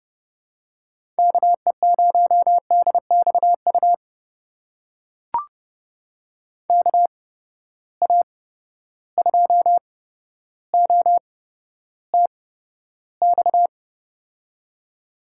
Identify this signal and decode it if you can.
Morse code, heard as a keyed tone, about 32 wpm, spaced as for 13 wpm, 700 Hz. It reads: KE0DXU KA2OTX
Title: USA Call Signs - 6 Characters - 8x Character Spacing - 30wpm